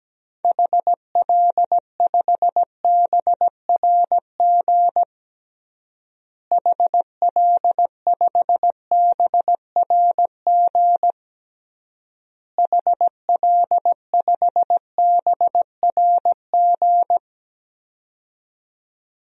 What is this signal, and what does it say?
Morse code, heard as a keyed tone, about 17 wpm, 700 Hz. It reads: HL5BRG HL5BRG HL5BRG
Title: DXCC-100 Call Signs - 6 Characters - Repeated 3x - 17wpm